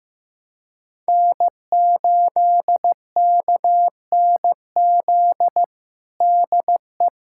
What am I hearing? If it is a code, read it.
Morse code, heard as a keyed tone, about 15 wpm, 700 Hz. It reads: N8KNZ DE
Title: USA Callsign DE Callsign - 5 Characters - 15wpm